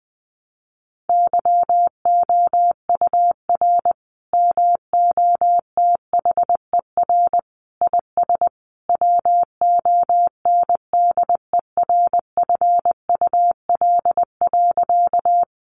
Morse code, heard as a keyed tone, about 20 wpm, 700 Hz. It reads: YOUR MOTHER IS WONDERFUL.
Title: Sentences from Top 1100 Words 20wpm